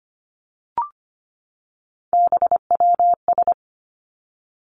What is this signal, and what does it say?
Morse code, heard as a keyed tone, about 25 wpm, 700 Hz. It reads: BWS
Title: Three Challenging Letter-Number Combinations 25wpm